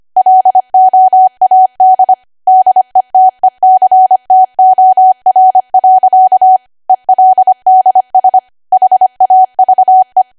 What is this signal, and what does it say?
Morse code, heard as a keyed tone, about 25 wpm, 750 Hz. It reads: LOAD DETECTOR. ELDS HAVE
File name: Archive / 25 WPM / 25WPM